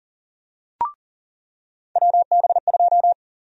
Morse code, heard as a keyed tone, about 40 wpm, 700 Hz. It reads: WB2
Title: USA Call Sign Prefixes - 40wpm